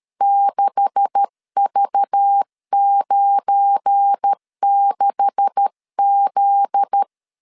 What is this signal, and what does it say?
Morse code, heard as a keyed tone, about 13 wpm, 800 Hz. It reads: 6V96Z
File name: LEZIONI / 9 286 12wpm 2305